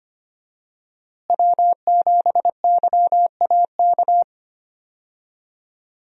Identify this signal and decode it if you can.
Morse code, heard as a keyed tone, about 25 wpm, 700 Hz. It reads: W7YAK
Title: USA Call Signs - 5 Characters - Repeated 3x - 25wpm